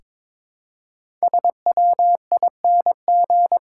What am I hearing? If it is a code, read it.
Morse code, heard as a keyed tone, about 22 wpm, 700 Hz. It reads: SWING
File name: 5 Letter Words - First 3 Letters Given - 22wpm